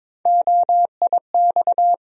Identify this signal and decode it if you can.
Morse code, heard as a keyed tone, about 22 wpm, 700 Hz. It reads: OIX
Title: USA Call Sign Postfixes - 3 Letters 22wpm